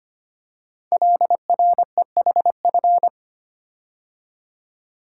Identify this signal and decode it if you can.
Morse code, heard as a keyed tone, about 25 wpm, 700 Hz. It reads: LREHF
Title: Code Groups - 5 Letters - 25wpm